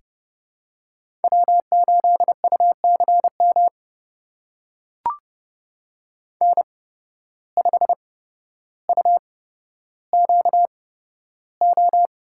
Morse code, heard as a keyed tone, about 32 wpm, spaced as for 13 wpm, 700 Hz. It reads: W8UCM N5UQO
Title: USA Call Signs - 5 Characters - 8x Character Spacing - 30wpm